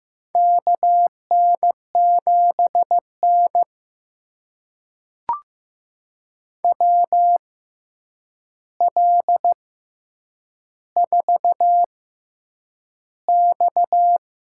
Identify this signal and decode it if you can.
Morse code, heard as a keyed tone, about 15 wpm, spaced as for 8 wpm, 700 Hz. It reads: KN7N WL4X
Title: USA Call Signs - 4 Characters - 6x Character Spacing - 15wpm